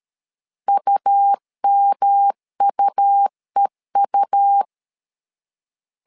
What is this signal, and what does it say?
Morse code, heard as a keyed tone, about 13 wpm, 800 Hz. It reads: UMUEU
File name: LEZIONI / 3 MWRO / 3 M 12wpm 1541